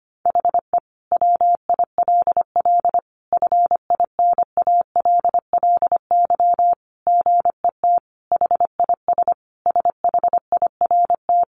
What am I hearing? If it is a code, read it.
Morse code, heard as a keyed tone, about 25 wpm, 700 Hz. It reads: HE WILL FINALLY GET HIS SHIRT
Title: Additional Easy Phrases 25wpm